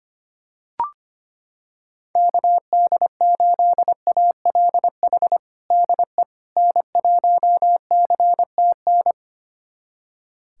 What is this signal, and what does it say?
Morse code, heard as a keyed tone, about 25 wpm, 700 Hz. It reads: KD8ALH DE N1CTN
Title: USA Callsign DE Callsign 25wpm